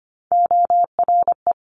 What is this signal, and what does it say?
Morse code, heard as a keyed tone, about 25 wpm, 700 Hz. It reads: ORE
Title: Top Three Letter Combinations in Words 25wpm